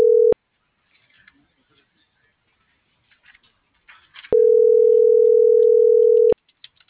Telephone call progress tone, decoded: ringback tone

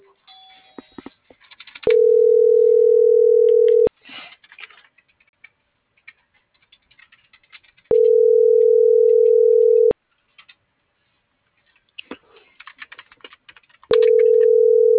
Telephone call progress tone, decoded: ringback tone

